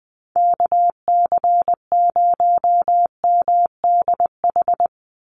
Morse code, heard as a keyed tone, about 20 wpm, 700 Hz. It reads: KC0MDH
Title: USA Call Signs 20wpm